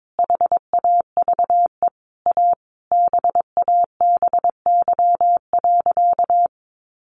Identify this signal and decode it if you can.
Morse code, heard as a keyed tone, about 22 wpm, 700 Hz. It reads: HAVE A BABY.
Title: Sentences from Top 1300 Words 22wpm